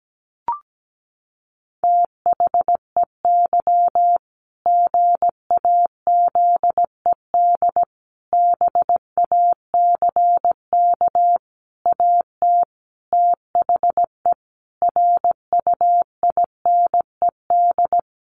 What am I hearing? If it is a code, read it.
Morse code, heard as a keyed tone, about 17 wpm, 700 Hz. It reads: THEY GAZED BACK AT THE RUINED
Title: A Sound Of Thunder 17wpm